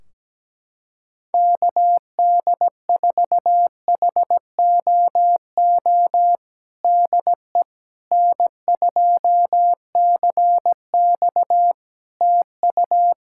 Morse code, heard as a keyed tone, about 17 wpm, 700 Hz. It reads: KD4HOO DE N2CX TU
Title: Practice QSO - Exchange 4 - Ending - 17wpm